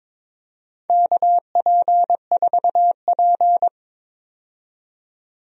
Morse code, heard as a keyed tone, about 22 wpm, 700 Hz. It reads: KP4P
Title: USA Call Signs - 4 Characters - 22wpm